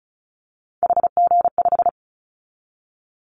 Morse code, heard as a keyed tone, about 34 wpm, 700 Hz. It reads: HG5
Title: Callsign Prefixes for Top 25 DXCC - Repeated 3x - 35wpm